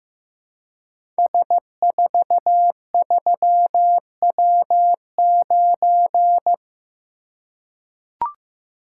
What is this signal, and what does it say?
Morse code, heard as a keyed tone, about 15 wpm, 700 Hz. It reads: S43W9
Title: Code Groups - 5 Letter:Numbers - 15wpm